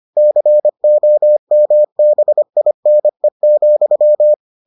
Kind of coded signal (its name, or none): Morse code